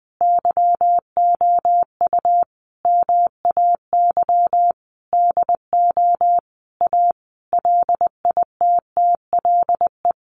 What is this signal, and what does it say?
Morse code, heard as a keyed tone, about 20 wpm, 700 Hz. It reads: YOU MAY DO A LITTLE
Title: Sentences from Top 900 Words 20wpm